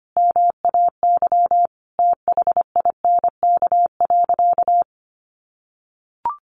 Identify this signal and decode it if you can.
Morse code, heard as a keyed tone, about 25 wpm, 700 Hz. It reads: MAY THINK.
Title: Sentences from Top 1100 Words 25wpm